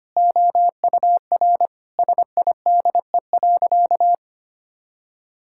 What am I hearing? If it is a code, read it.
Morse code, heard as a keyed tone, about 25 wpm, 700 Hz. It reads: OUR SIDE.
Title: Sentences from Top 1000 Words - Review Part 6 - 25wpm